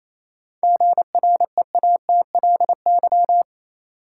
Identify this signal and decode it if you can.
Morse code, heard as a keyed tone, about 28 wpm, 700 Hz. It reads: GREATLY
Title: Sentences from Top 1400 Words - New Word Drill - 28wpm